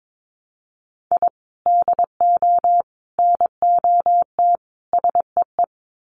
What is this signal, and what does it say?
Morse code, heard as a keyed tone, about 22 wpm, 700 Hz. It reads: I DO NOT SEE
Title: Sets of 4 Words 22wpm